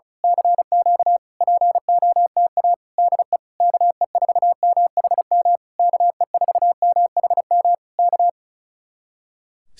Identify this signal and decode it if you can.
Morse code, heard as a keyed tone, about 32 wpm, 700 Hz. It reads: CQ POTA DE KE4MHM KE4MHM K
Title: Speed-Racing Common QSO Phrases Encore - 35wpm